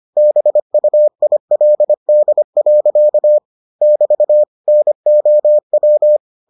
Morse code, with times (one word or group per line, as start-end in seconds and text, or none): 0.2-3.4 BUILD.
3.8-4.4 <BT>
4.7-6.2 NOW